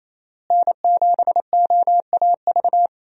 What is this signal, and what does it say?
Morse code, heard as a keyed tone, about 28 wpm, 700 Hz. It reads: N7OAV
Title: USA Call Signs - 5 Characters - Repeated 2x - 28wpm